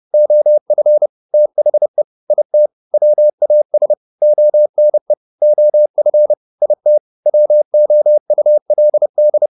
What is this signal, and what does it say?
Morse code, heard as a keyed tone, about 30 wpm, 600 Hz. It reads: OF THE IT WAS ONE OF IT WOULD